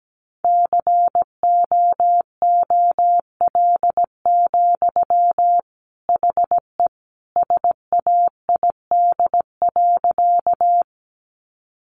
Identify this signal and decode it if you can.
Morse code, heard as a keyed tone, about 17 wpm, 700 Hz. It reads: COOL, HE SAID.
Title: Sentences from Top 1200 Words 17wpm